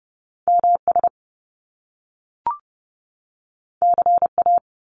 Morse code, heard as a keyed tone, about 30 wpm, 700 Hz. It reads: MS CA
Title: US States 30wpm